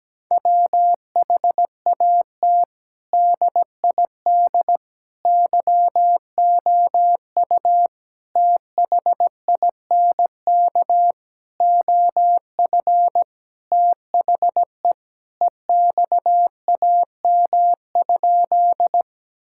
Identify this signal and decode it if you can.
Morse code, heard as a keyed tone, about 17 wpm, 700 Hz. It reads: WHAT DID YOU THINK OF THE EXAM?
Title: Sentences from Top 1000 Words - Review Part 5 - 17wpm